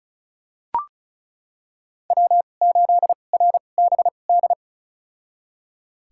Morse code, heard as a keyed tone, about 34 wpm, spaced as for 25 wpm, 700 Hz. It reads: W8RBD
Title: USA Call Signs - 5 Characters - 2x Character Spacing - 35wpm